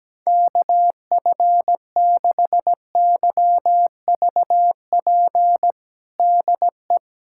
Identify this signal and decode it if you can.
Morse code, heard as a keyed tone, about 17 wpm, 700 Hz. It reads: KF6YVP DE
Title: USA Callsign DE Callsign 17wpm